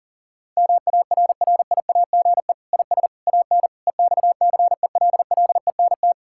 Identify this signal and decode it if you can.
Morse code, heard as a keyed tone, about 40 wpm, 700 Hz. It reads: MARRIAGE IS AN EXCELLENT
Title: Sentences from Top 1000 Words - Review Part 5 - 40wpm